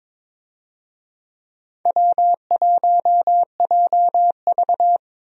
Morse code, heard as a keyed tone, about 22 wpm, 700 Hz. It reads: W1JV
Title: USA Call Signs - 4 Characters - 22wpm